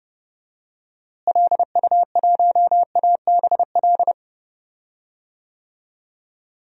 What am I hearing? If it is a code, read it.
Morse code, heard as a keyed tone, about 31 wpm, 700 Hz. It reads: LU1ABL
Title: DXCC-100 Call Signs - 6 Characters - Repeated 3x - 30wpm